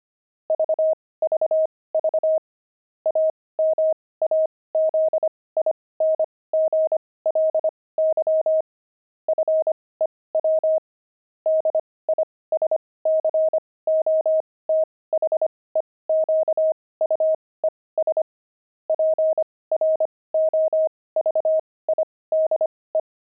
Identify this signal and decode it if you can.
Morse code, heard as a keyed tone, about 25 wpm, spaced as for 18 wpm, 630 Hz. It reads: VVV AMAZINGLY FEW DISCOTHEQUES PROVIDE
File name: pangram-fs-18-cs-25-disco